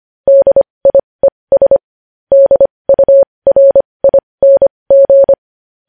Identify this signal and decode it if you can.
Morse code, heard as a keyed tone, about 25 wpm, spaced as for 20 wpm, 550 Hz. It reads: DIES DURING
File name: podcast / News headlines-1x-25x20